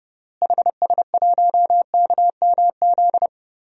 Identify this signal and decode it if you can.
Morse code, heard as a keyed tone, about 32 wpm, 700 Hz. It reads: HS1KMZ
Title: DXCC-100 Call Signs - 6 Characters - Repeated 2x - 30wpm